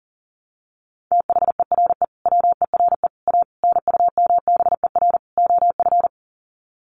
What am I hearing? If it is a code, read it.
Morse code, heard as a keyed tone, about 40 wpm, 700 Hz. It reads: THERE WERE A NUMBER OF
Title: Sets of 5 Words Encore2 40wpm